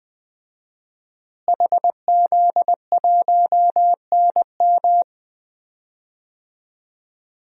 Morse code, heard as a keyed tone, about 20 wpm, 700 Hz. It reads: HZ1NM
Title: DXCC-100 Call Signs - 5 Characters - Repeated 2x - 20wpm